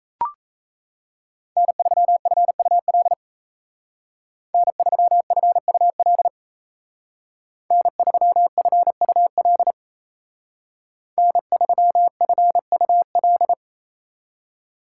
Morse code, heard as a keyed tone, about 33 wpm, 700 Hz. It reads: N3FUL N3FUL N3FUL N3FUL
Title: Speed Racing - USA 1x3 Call Signs 28wpm